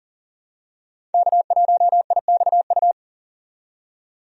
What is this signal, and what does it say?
Morse code, heard as a keyed tone, about 38 wpm, 700 Hz. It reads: K1IXU
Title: USA Call Signs - 5 Characters - 40wpm